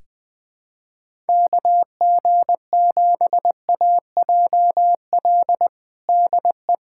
Morse code, heard as a keyed tone, about 20 wpm, 700 Hz. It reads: KG7AJL DE
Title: USA Callsign DE Callsign - 6 Characters - 20wpm